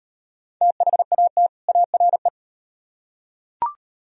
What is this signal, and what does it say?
Morse code, heard as a keyed tone, about 38 wpm, 700 Hz. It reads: THAT ARE